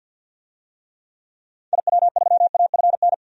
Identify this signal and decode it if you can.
Morse code, heard as a keyed tone, about 50 wpm, 700 Hz. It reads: IW3AFN
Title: DXCC-100 Call Signs - 6 Characters - 50wpm